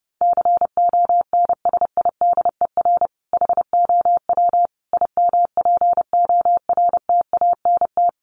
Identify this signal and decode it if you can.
Morse code, heard as a keyed tone, about 30 wpm, 700 Hz. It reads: CONSIDER HOW IMPORTANT
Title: Sentences from Top 1000 Words 30wpm